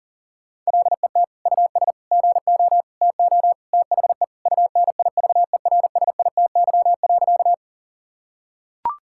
Morse code, heard as a keyed tone, about 40 wpm, 700 Hz. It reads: LET US GO TO THE UNIVERSITY.
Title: Sentences from Top 1000 Words - Review Part 5 - 40wpm